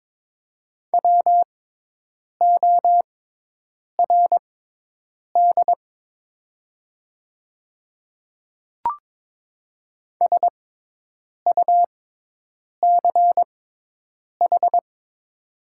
Morse code, heard as a keyed tone, about 22 wpm, spaced as for 8 wpm, 700 Hz. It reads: WORD SUCH
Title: 6x Character Spacing - Top 300 Words - 22wpm